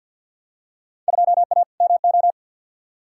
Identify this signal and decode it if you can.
Morse code, heard as a keyed tone, about 51 wpm, 700 Hz. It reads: 2A MO
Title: Field Day Exchanges Encore - 50wpm